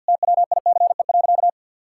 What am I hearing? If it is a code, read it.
Morse code, heard as a keyed tone, about 50 wpm, 700 Hz. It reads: TWICE.
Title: Sentences from Top 1600 Words 50wpm